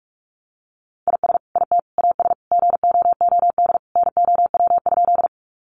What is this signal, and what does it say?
Morse code, heard as a keyed tone, about 44 wpm, 700 Hz. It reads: IS IT AS GOOD NOW?
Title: Sentences from Top 200 Words 45wpm